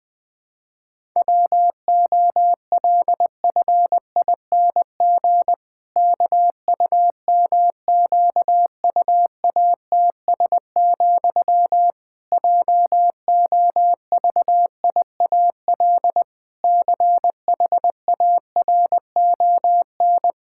Morse code, heard as a keyed tone, about 20 wpm, 700 Hz. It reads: WOLFING KUMQUATS, JOVIAL CHARON
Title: Pangram Challenge v2 - 20wpm